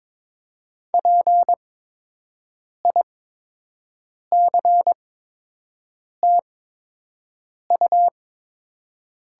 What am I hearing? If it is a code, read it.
Morse code, heard as a keyed tone, about 22 wpm, spaced as for 6 wpm, 700 Hz. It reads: PICTU
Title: 8x Character Spacing - Top 300 Words - 22wpm